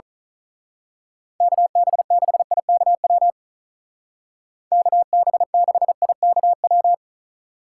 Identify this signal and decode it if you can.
Morse code, heard as a keyed tone, about 39 wpm, 700 Hz. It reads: KB6IKW KB6IKW
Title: Speed Racing - USA 2x3 Call Signs 35wpm